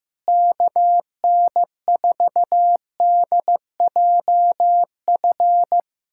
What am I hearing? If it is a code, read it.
Morse code, heard as a keyed tone, about 15 wpm, 700 Hz. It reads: KN4DJF
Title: USA Callsign DE Callsign - 6 Characters - 15wpm